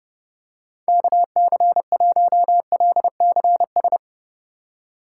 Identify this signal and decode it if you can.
Morse code, heard as a keyed tone, about 32 wpm, 700 Hz. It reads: KC1LCS
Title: USA Call Signs - 6 Characters - 2x Character Spacing - 30wpm